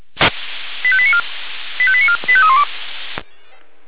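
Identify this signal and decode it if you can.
Five-tone selective call (ZVEI), heard as quick runs of five tones, tones of about 70 ms. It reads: 85894 85894 85312
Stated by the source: Five-tone ZVEI selective call 85320 received on an SDR.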